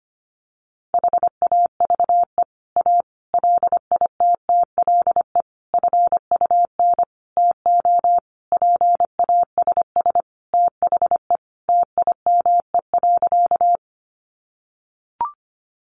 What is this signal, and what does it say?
Morse code, heard as a keyed tone, about 25 wpm, 700 Hz. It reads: HAVE A LITTLE FUN TO PASS THE TIME.
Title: Sentences from Top 700 Words 25wpm